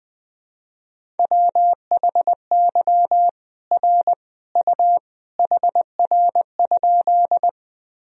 Morse code, heard as a keyed tone, about 20 wpm, 700 Hz. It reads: WHY R U HR?
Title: Sentences from Top 200 Words and Abbreviations 20wpm